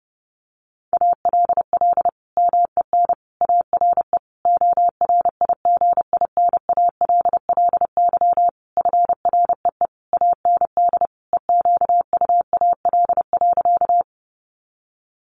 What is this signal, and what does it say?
Morse code, heard as a keyed tone, about 30 wpm, 700 Hz. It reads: ALL MEN ARE ORIGINALLY FREE AND EQUAL.
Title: Sentences from Top 1500 Words 30wpm